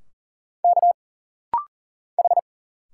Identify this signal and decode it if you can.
Morse code, heard as a keyed tone, about 40 wpm, 700 Hz. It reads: K H
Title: Single Letters - Rapid-Fire 40wpm